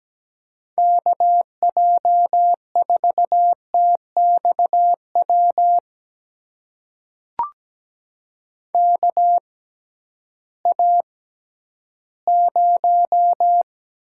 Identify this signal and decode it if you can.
Morse code, heard as a keyed tone, about 17 wpm, spaced as for 11 wpm, 700 Hz. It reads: KJ4TXW KA0
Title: USA Call Signs - 6x Character Spacing - 17wpm